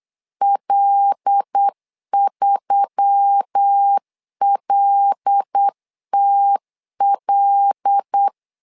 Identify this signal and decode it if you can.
Morse code, heard as a keyed tone, about 8 wpm, 800 Hz. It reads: L3LTL
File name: LEZIONI / 5 LBVC / 5 L 8wpm 2831